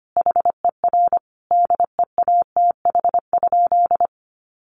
Morse code, heard as a keyed tone, about 25 wpm, 700 Hz. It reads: HER DEATH?
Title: Sentences from Top 1500 Words 25wpm